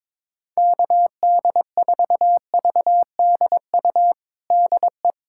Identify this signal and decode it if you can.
Morse code, heard as a keyed tone, about 22 wpm, 700 Hz. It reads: KD4VDU DE
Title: USA Callsign DE Callsign - 6 Characters - 22wpm